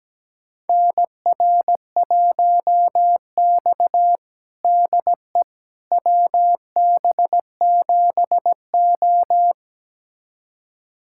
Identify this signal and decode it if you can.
Morse code, heard as a keyed tone, about 17 wpm, 700 Hz. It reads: NR1X DE WB7O
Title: USA Callsign DE Callsign - 4 Characters - 17wpm